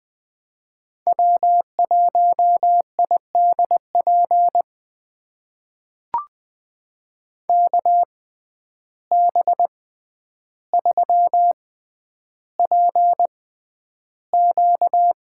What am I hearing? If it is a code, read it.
Morse code, heard as a keyed tone, about 20 wpm, spaced as for 10 wpm, 700 Hz. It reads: W1IDP KB3PQ
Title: USA Call Signs - 6x Character Spacing - 20wpm